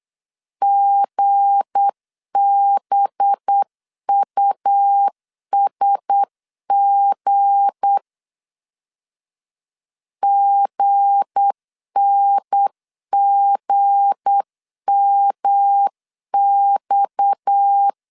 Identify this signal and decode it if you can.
Morse code, heard as a keyed tone, about 8 wpm, 800 Hz. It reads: GBUSG GNGMX
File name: LEZIONI / 7 GF9Z / 7 G 8wpm 3118